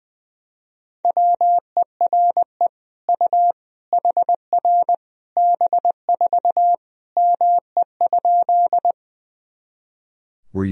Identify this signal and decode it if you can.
Morse code, heard as a keyed tone, about 20 wpm, 700 Hz. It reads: WERE U HR B4 ME?
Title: Sentences from Top 200 Words and Abbreviations 20wpm